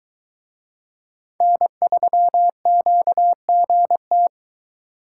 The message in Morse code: N3QGT

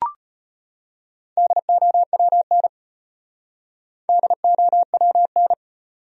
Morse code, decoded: DOWN DOWN

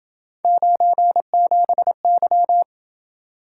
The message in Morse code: 97Y

